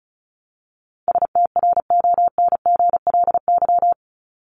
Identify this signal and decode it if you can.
Morse code, heard as a keyed tone, about 34 wpm, 700 Hz. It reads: STRONGLY
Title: LY Suffix - Top 100 Words with Base Words 35wpm